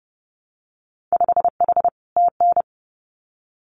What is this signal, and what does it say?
Morse code, heard as a keyed tone, about 30 wpm, 700 Hz. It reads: 5H TN
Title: Winter Field Day Exchanges Encore 30wpm